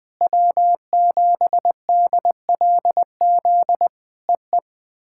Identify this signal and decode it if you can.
Morse code, heard as a keyed tone, about 20 wpm, 700 Hz. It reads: W7DLZ EE
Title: POTA QSO Practice - 20wpm